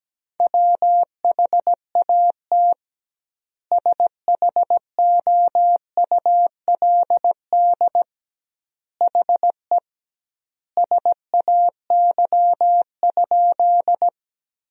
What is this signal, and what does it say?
Morse code, heard as a keyed tone, about 17 wpm, 700 Hz. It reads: WHAT SHOULD HE SAY?
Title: 2x Word-Spacing - Sentences from Top 300 Words - 17wpm